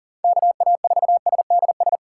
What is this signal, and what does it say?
Morse code, heard as a keyed tone, about 40 wpm, 700 Hz. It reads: KA4SDS
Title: USA Callsign DE Callsign 40wpm